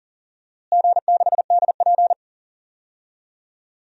Morse code, heard as a keyed tone, about 40 wpm, 700 Hz. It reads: G6DP
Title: DXCC-100 Call Signs - 4 Characters - Repeated 3x - 40wpm